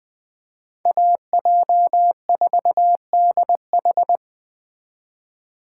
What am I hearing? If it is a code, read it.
Morse code, heard as a keyed tone, about 20 wpm, 700 Hz. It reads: AJ4DH
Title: Speed Racing - USA 2x2 Call Signs 17wpm